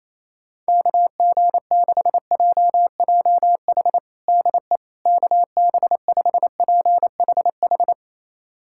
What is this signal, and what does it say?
Morse code, heard as a keyed tone, about 28 wpm, 700 Hz. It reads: KG6JJH DE KB5PHH
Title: USA Callsign DE Callsign 28wpm